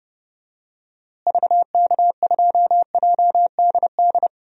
Morse code, heard as a keyed tone, about 30 wpm, 700 Hz. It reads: VK2JDD
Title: DXCC-100 Call Signs - 6 Characters - Repeated 2x - 30wpm